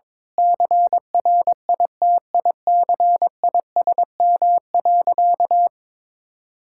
Morse code, heard as a keyed tone, about 22 wpm, 700 Hz. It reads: CRITICISM.
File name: practice / 22wpm / Sentences with 8-Word Phrases - 22wpm